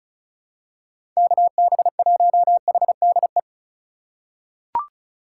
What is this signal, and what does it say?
Morse code, heard as a keyed tone, about 34 wpm, 700 Hz. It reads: KB1HDE
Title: USA Call Signs - 6 Characters - 2x Character Spacing - 35wpm